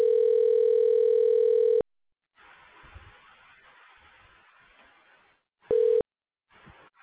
Ringback tone (North America), 440 + 480 Hz.